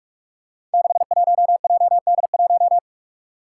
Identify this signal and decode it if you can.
Morse code, heard as a keyed tone, about 46 wpm, 700 Hz. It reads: 61JD1
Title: Five Challenging Letter-Number Combinations 45wpm